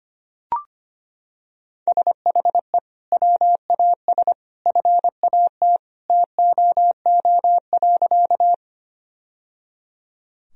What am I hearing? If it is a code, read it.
Morse code, heard as a keyed tone, about 25 wpm, 700 Hz. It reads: SHE WAS FAT TOO.
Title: Sentences from Top 1000 Words - Review Part 4 - 25wpm